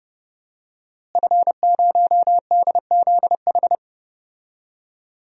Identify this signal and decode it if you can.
Morse code, heard as a keyed tone, about 30 wpm, 700 Hz. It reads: F0DZH
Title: DXCC-100 Call Signs - 5 Characters - Repeated 2x - 30wpm